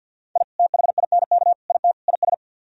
Morse code, heard as a keyed tone, about 61 wpm, 700 Hz. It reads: I THINK IT IS
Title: Speed-Racing - Sets of 4 Words 50wpm